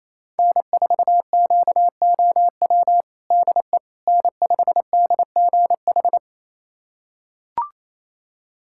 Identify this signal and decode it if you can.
Morse code, heard as a keyed tone, about 28 wpm, 700 Hz. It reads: N4QOW DE N5DGH
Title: USA Callsign DE Callsign - 5 Characters - 28wpm